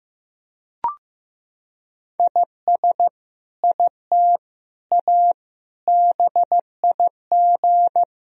Morse code, heard as a keyed tone, about 15 wpm, 700 Hz. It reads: IS IT A BIG
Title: Sentences from Top 1000 Words - Review Part 4 - 15wpm